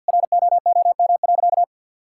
Morse code, heard as a keyed tone, about 52 wpm, 700 Hz. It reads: ROOM.
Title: Sentences from Top 1000 Words - Review Part 6 - 50wpm